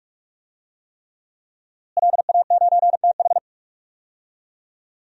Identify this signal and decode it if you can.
Morse code, heard as a keyed tone, about 46 wpm, 700 Hz. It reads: LA9TH
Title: DXCC-100 Call Signs - 5 Characters - Repeated 3x - 45wpm